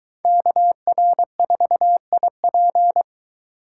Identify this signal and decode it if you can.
Morse code, heard as a keyed tone, about 23 wpm, 700 Hz. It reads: KR4IP